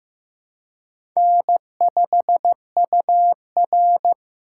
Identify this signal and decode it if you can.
Morse code, heard as a keyed tone, about 15 wpm, 700 Hz. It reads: N5UR
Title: USA Call Signs - 4 Characters - 2x Character Spacing - 15wpm